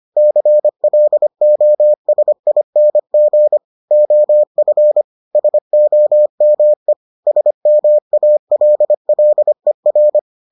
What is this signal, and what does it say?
Morse code, heard as a keyed tone, about 25 wpm, 600 Hz. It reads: CLOSING OF SOME SMALLER